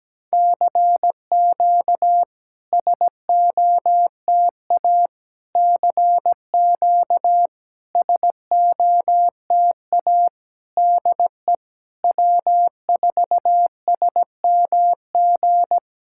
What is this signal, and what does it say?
Morse code, heard as a keyed tone, about 17 wpm, 700 Hz. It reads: CQ SOTA CQ SOTA DE W4SMG
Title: SOTA QSO Practice 17wpm